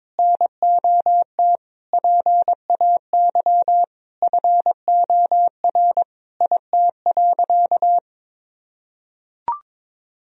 Morse code, heard as a keyed tone, about 22 wpm, 700 Hz. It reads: NOT PAY FOR IT.
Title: Sentences from Top 1000 Words - Review Part 3 - 22wpm